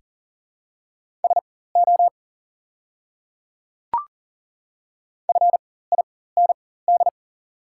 Morse code, heard as a keyed tone, about 39 wpm, spaced as for 19 wpm, 700 Hz. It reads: SO FIND
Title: 4x Character Spacing - Top 100 Words - 40wpm